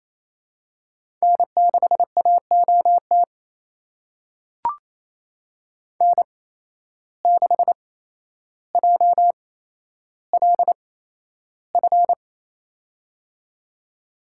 Morse code, heard as a keyed tone, about 28 wpm, spaced as for 12 wpm, 700 Hz. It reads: N6AOT N6JLF
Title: USA Call Signs - 5 Characters - 8x Character Spacing - 28wpm